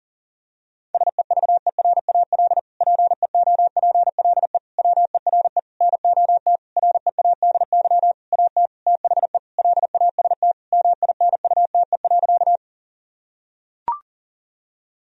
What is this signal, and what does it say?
Morse code, heard as a keyed tone, about 40 wpm, 700 Hz. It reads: SEVERAL PEOPLE WERE NOT READY AT THE LAST MINUTE.
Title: Sentences from Top 1000 Words - Review Part 2 - 40wpm